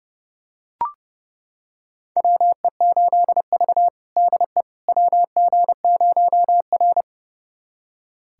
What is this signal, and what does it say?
Morse code, heard as a keyed tone, about 30 wpm, 700 Hz. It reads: WE8V DE WG0R
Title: USA Callsign DE Callsign - 4 Characters - 30wpm